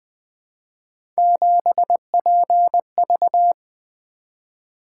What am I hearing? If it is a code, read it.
Morse code, heard as a keyed tone, about 20 wpm, 700 Hz. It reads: 7PV
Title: Three Letter-Number Combinations - 20wpm